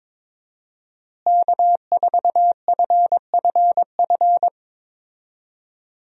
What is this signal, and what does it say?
Morse code, heard as a keyed tone, about 22 wpm, 700 Hz. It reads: K4FFF
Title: USA Call Signs - 5 Characters - Repeated 3x - 22wpm